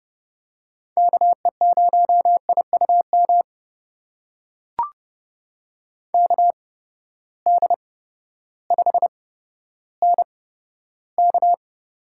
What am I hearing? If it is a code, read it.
Morse code, heard as a keyed tone, about 30 wpm, spaced as for 14 wpm, 700 Hz. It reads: KE0IUM KD5NK
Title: USA Call Signs - 6 Characters - 8x Character Spacing - 30wpm